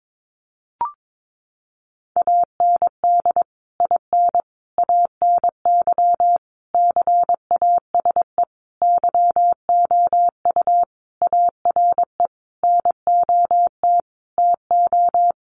Morse code, heard as a keyed tone, about 22 wpm, 700 Hz. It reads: AND IN ANY CASE YOU ARE NOT TO
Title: Sentences from Top 1200 Words 22wpm